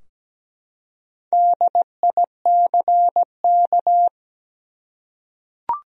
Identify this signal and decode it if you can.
Morse code, heard as a keyed tone, about 17 wpm, 700 Hz. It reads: DICK
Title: USA Common Names - 17wpm